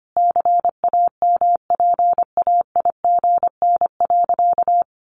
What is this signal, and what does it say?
Morse code, heard as a keyed tone, about 25 wpm, 700 Hz. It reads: CAMPAIGN.
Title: Sentences from Top 1200 Words 25wpm